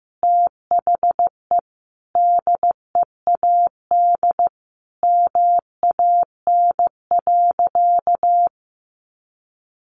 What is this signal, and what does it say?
Morse code, heard as a keyed tone, about 15 wpm, 700 Hz. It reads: THE DEAD MAN.
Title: Sentences from Top 1500 Words 15wpm